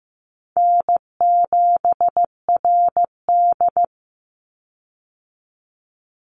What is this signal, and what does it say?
Morse code, heard as a keyed tone, about 15 wpm, 700 Hz. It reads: N7RD
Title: Easy Call Signs - 15wpm